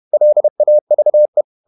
Morse code, heard as a keyed tone, about 31 wpm, 600 Hz. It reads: LAVE
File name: Ferie QRQ 31wpm